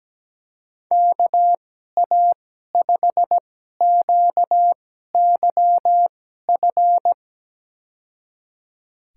Morse code, heard as a keyed tone, about 17 wpm, spaced as for 12 wpm, 700 Hz. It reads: KA5QYF
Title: USA Call Signs - 6 Characters - 2x Character Spacing - 17wpm